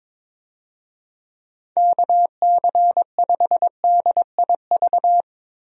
Morse code, heard as a keyed tone, about 22 wpm, 700 Hz. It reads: KC5DIV
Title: USA Call Signs - 6 Characters - Repeated 2x - 22wpm